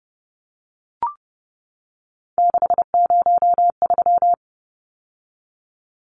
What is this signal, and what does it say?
Morse code, heard as a keyed tone, about 30 wpm, 700 Hz. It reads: 603
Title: Numbers - 3-Digit 30wpm